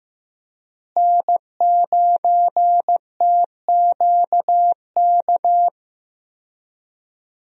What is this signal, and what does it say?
Morse code, heard as a keyed tone, about 15 wpm, 700 Hz. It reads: N9TQK
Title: USA Call Signs - 5 Characters - Repeated 3x - 15wpm